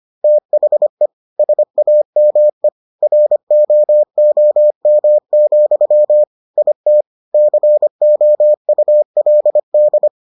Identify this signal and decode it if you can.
Morse code, heard as a keyed tone, about 25 wpm, 600 Hz. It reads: THE SAME ROOM, IT COULD